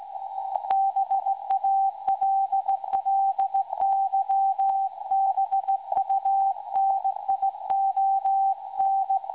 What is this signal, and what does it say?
Morse code, heard as a keyed tone, about 17 wpm, 780 Hz. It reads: BALLYBUNION
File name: WebSDR records / SAQ recording 17.2kHz